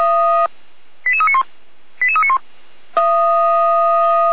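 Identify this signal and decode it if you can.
Five-tone selective call (ZVEI), heard as quick runs of five tones, tones of about 70 ms. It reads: 88391 88391
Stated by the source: ZVEI five-tone selective call received via SDR, address 88303.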